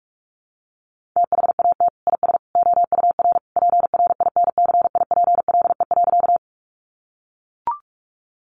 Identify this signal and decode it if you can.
Morse code, heard as a keyed tone, about 44 wpm, 700 Hz. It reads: THAT IS OUR PRINCIPLE.
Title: Sentences from Top 1400 Words 45wpm